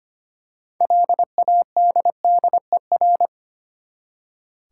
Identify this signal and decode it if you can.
Morse code, heard as a keyed tone, about 25 wpm, 690 Hz. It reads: LADDER
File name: practice / 25wpm / Easy Words with Hints - 25wpm